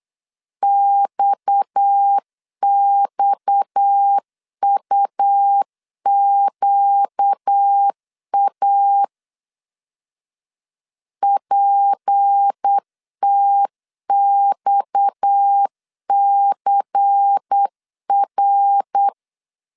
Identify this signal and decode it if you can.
Morse code, heard as a keyed tone, about 8 wpm, 800 Hz. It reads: XXUQA PTXCR
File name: CW / DET / 1 6 1500C S B 8wpm 5142